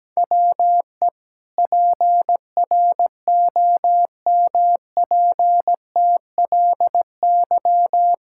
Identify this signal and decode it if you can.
Morse code, heard as a keyed tone, about 17 wpm, 700 Hz. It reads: WE PROMPTLY
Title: Pangram Challenge v2 - 17wpm